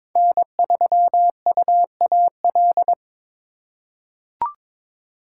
Morse code, heard as a keyed tone, about 22 wpm, 700 Hz. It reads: N3UAL
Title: USA Call Signs - 5 Characters - 4x Character Spacing - 22wpm